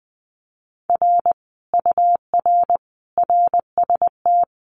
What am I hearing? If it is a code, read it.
Morse code, heard as a keyed tone, about 20 wpm, 700 Hz. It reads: R UR RST
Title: Common QSO Phrases Encore 20wpm